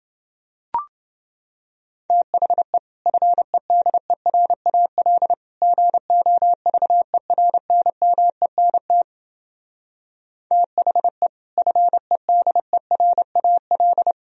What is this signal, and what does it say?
Morse code, heard as a keyed tone, about 29 wpm, 700 Hz. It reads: THE FEDERAL GOVERNMENT THE FEDERAL